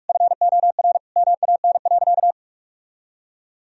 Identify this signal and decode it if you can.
Morse code, heard as a keyed tone, about 46 wpm, 700 Hz. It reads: FOR MAN.
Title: Sentences from Top 1000 Words - Review Part 3 - 45wpm